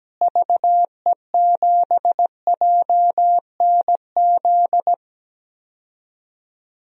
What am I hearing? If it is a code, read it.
Morse code, heard as a keyed tone, about 17 wpm, 700 Hz. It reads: VE7JNZ
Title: DXCC-100 Call Signs - 6 Characters - Repeated 2x - 17wpm